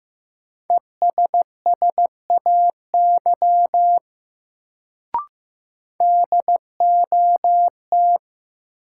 Morse code, heard as a keyed tone, about 15 wpm, 700 Hz. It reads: ESSAY DOT
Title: Sentences from Top 1500 Words - New Word Drill - 15wpm